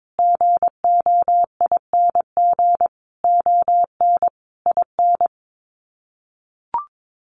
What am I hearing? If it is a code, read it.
Morse code, heard as a keyed tone, about 22 wpm, 700 Hz. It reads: GOING ON IN
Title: Sets of 5 Words Encore 22wpm